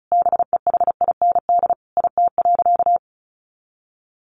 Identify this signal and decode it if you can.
Morse code, heard as a keyed tone, about 34 wpm, 700 Hz. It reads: BEHIND IT.
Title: Sentences from Top 500 Words 35wpm